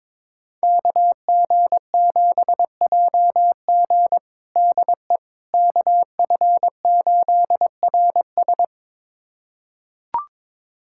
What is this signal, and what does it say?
Morse code, heard as a keyed tone, about 22 wpm, 700 Hz. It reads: KG7JG DE KF8RS
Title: USA Callsign DE Callsign - 5 Characters - 22wpm